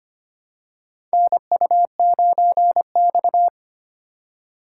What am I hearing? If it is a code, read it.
Morse code, heard as a keyed tone, about 25 wpm, 700 Hz. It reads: NU9X
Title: USA Call Signs - 4x Character Spacing - 25wpm